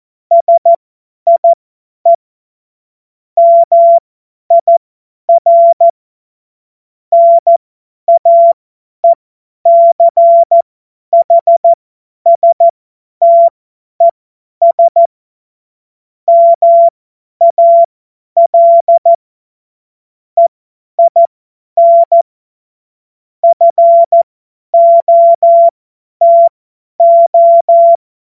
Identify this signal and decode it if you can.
Morse code, heard as a keyed tone, about 14 wpm, spaced as for 10 wpm, 680 Hz. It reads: SIE MIR NAECHSTES MAL EIN FOTO
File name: Morsekurs / Witze / Witz 02 April 2024 50 F70